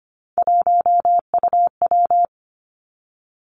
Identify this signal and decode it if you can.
Morse code, heard as a keyed tone, about 25 wpm, 700 Hz. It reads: 1UW
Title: Three Challenging Letter-Number Combinations 25wpm